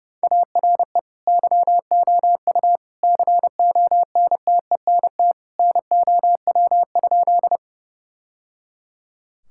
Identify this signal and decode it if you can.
Morse code, heard as a keyed tone, about 30 wpm, 700 Hz. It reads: ARE YOU CONTENT NOW?
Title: Sentences from Top 1000 Words - Review Part 6 - 30wpm